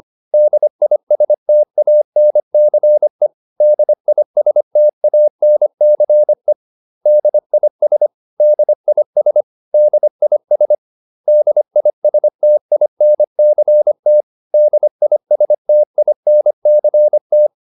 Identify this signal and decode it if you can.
Morse code, heard as a keyed tone, about 25 wpm, 600 Hz. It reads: DISTANCE DISTANCE DIS DIS DIS DISTINCT DISTINCT
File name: DIS8 25